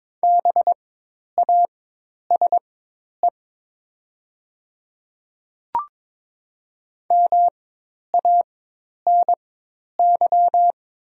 Morse code, heard as a keyed tone, about 22 wpm, spaced as for 10 wpm, 700 Hz. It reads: BASE MANY
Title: 4x Character Spacing - Top 300 Words - 22wpm